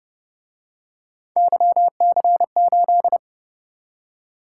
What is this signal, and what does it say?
Morse code, heard as a keyed tone, about 28 wpm, 700 Hz. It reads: YC8